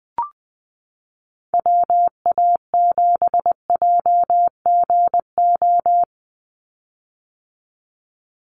Morse code, heard as a keyed tone, about 20 wpm, 700 Hz. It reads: WA7JGO
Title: USA Call Signs 20wpm